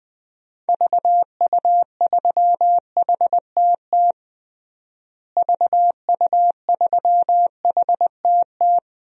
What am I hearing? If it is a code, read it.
Morse code, heard as a keyed tone, about 20 wpm, 700 Hz. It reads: VU3HTT VU3HTT
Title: DXCC-100 Call Signs - 6 Characters - Repeated 2x - 20wpm